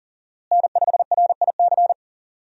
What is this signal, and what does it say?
Morse code, heard as a keyed tone, about 40 wpm, 700 Hz. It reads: N5RIC
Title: USA Call Signs - 5 Characters - Repeated 3x - 40wpm